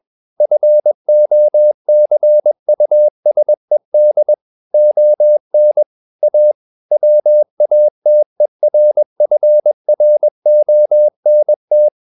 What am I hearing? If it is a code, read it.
Morse code, heard as a keyed tone, about 21 wpm, 600 Hz. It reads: FOCUSED ON A WATERFRONT
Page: CW med Gnister